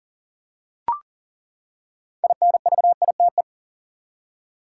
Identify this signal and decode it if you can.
Morse code, heard as a keyed tone, about 40 wpm, 700 Hz. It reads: INVITE
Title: Six Letter Words - 40wpm